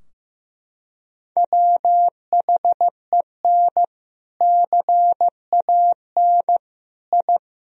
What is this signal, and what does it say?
Morse code, heard as a keyed tone, about 15 wpm, 700 Hz. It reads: WHEN CAN I
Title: Sentences from Top 1000 Words - Review Part 3 - 15wpm